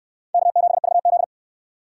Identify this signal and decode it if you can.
Morse code, heard as a keyed tone, about 63 wpm, 700 Hz. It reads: K7VZ